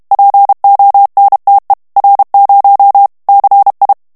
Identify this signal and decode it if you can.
Morse code, heard as a keyed tone, about 32 wpm, 800 Hz. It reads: PONTE R0 CI
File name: ITALIEN texte clair 1 vitesse 30 mots minute